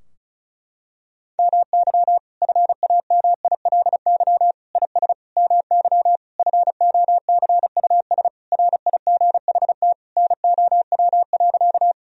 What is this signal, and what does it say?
Morse code, heard as a keyed tone, about 36 wpm, 700 Hz. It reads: MY FAMILY IS MY FOCUS RIGHT NOW.
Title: Sentences from Top 1000 Words - Review Part 4 - 35wpm